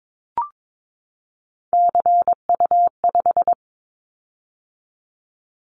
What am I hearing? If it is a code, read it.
Morse code, heard as a keyed tone, about 22 wpm, 700 Hz. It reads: CU5
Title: Callsign Prefixes for Top 50 DXCC - 22wpm